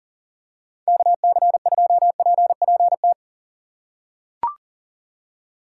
Morse code, heard as a keyed tone, about 38 wpm, 700 Hz. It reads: KC2PPT
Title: USA Call Signs - 6 Characters - 2x Character Spacing - 40wpm